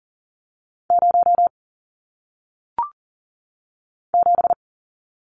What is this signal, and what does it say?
Morse code, heard as a keyed tone, about 40 wpm, 700 Hz. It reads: 0 7
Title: Numbers - 1-Digit 40wpm